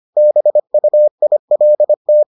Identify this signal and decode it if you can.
Morse code, heard as a keyed tone, about 25 wpm, 600 Hz. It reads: BUILT